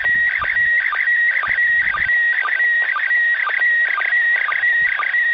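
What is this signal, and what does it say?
SSTV (slow-scan television), PD 120.
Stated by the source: SSTV transmission from the ISS.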